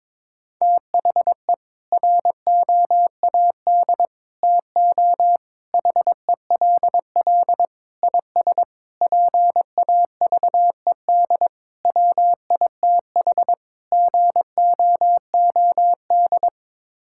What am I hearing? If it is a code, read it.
Morse code, heard as a keyed tone, about 22 wpm, 700 Hz. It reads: THE ROAD TO HELL IS PAVED WITH GOOD
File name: practice / 22wpm / Sets of 8 Words - 22wpm